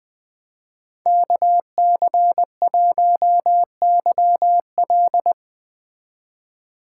Morse code, heard as a keyed tone, about 20 wpm, 700 Hz. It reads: KC1YL
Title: 2023 POTA Activator Call Signs - 20wpm